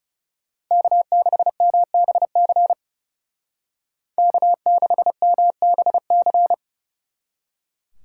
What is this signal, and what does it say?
Morse code, heard as a keyed tone, about 39 wpm, 700 Hz. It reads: K6MBC K6MBC